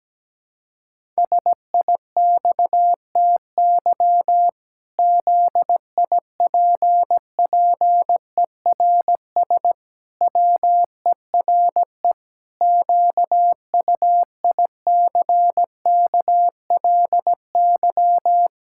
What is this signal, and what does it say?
Morse code, heard as a keyed tone, about 17 wpm, 700 Hz. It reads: SIXTY ZIPPERS WERE QUICKLY
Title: Pangram Challenge v2 - 17wpm